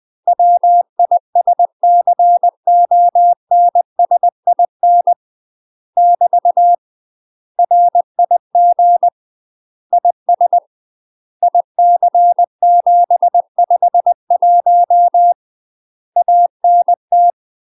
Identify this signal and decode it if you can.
Morse code, heard as a keyed tone, about 20 wpm, 700 Hz. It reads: WISCONSIN <BT> RIG IS IC751 ANT
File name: UK MASTER QSOs TRAINING / 20WPM / UK MASTER QSOs 0006